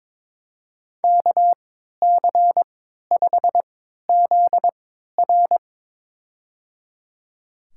Morse code, heard as a keyed tone, about 22 wpm, spaced as for 12 wpm, 700 Hz. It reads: KC5ZR
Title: USA Call Signs - 5 Characters - 3x Character Spacing - 22wpm